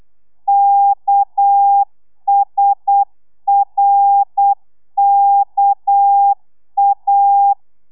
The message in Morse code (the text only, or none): KSRKA